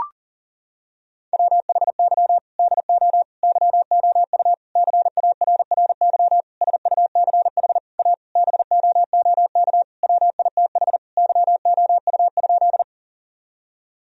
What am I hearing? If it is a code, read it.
Morse code, heard as a keyed tone, about 40 wpm, 700 Hz. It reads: WHY DO YOU CARRY SUCH A BOOK WITH YOU?
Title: Sentences from Top 1000 Words - Review Part 1 - 40wpm